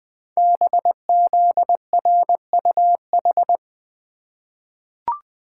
Morse code, heard as a keyed tone, about 20 wpm, 700 Hz. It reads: BZRUH
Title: Code Groups - 5 Letters - 20wpm